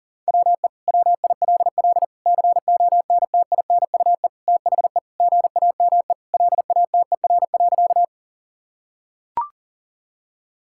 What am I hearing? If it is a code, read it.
Morse code, heard as a keyed tone, about 37 wpm, 700 Hz. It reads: WE WILL CONTINUE THE GAME LATER.
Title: Sentences from Top 1000 Words - Review Part 6 - 40wpm